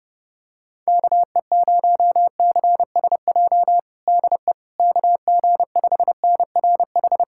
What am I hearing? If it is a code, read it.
Morse code, heard as a keyed tone, about 31 wpm, 700 Hz. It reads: KE0CSJ DE KG5NRH
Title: USA Callsign DE Callsign - 6 Characters - 30wpm